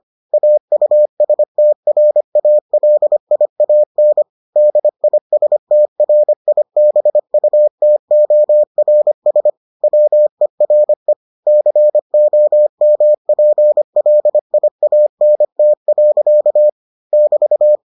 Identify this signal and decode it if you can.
Morse code, heard as a keyed tone, about 25 wpm, 600 Hz. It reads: AUSTRALIAN DISTRIBUTORS WERE COMPLIANT. <BT>